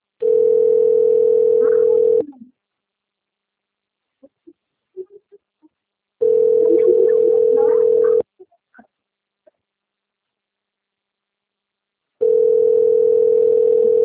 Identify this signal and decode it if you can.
Ringback tone (North America), 440 + 480 Hz.
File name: rg-1600-unknown- - - .9645